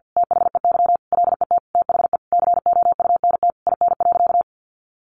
Morse code, heard as a keyed tone, about 50 wpm, 700 Hz. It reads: THEY LET THE COUNT IN.
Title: Sentences from Top 1500 Words 50wpm